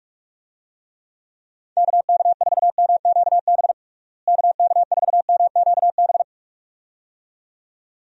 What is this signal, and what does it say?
Morse code, heard as a keyed tone, about 46 wpm, 700 Hz. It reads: KK4MQB KK4MQB
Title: USA Call Signs - 6 Characters - Repeated 2x - 45wpm